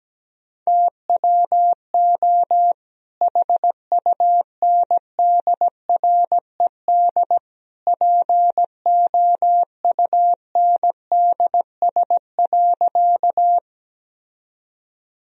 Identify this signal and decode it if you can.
Morse code, heard as a keyed tone, about 17 wpm, 700 Hz. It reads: TWO HUNDRED POUNDS.
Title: Sentences from Top 1000 Words - Review Part 2 - 17wpm